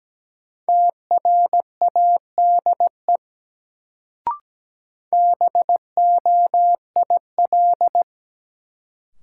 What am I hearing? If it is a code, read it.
Morse code, heard as a keyed tone, about 17 wpm, 700 Hz. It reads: TRADE BOIL
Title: Sets of 5 Words Encore - New Word Drill - 17wpm